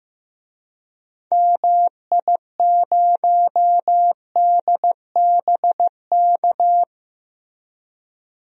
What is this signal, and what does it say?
Morse code, heard as a keyed tone, about 15 wpm, 700 Hz. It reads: MI0DBK
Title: DXCC-100 Call Signs - 6 Characters - Repeated 2x - 15wpm